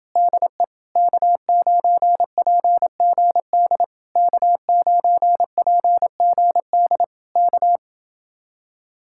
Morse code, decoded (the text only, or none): DE K9PGD K9PGD K